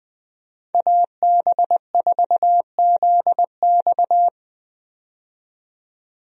Morse code, heard as a keyed tone, about 20 wpm, 700 Hz. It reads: AB4ZX
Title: USA Call Signs - 5 Characters - 20wpm